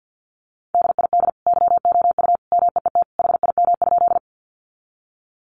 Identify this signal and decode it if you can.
Morse code, heard as a keyed tone, about 48 wpm, 700 Hz. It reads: DID YOU MEET HIM?
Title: Sentences from Top 1100 Words 50wpm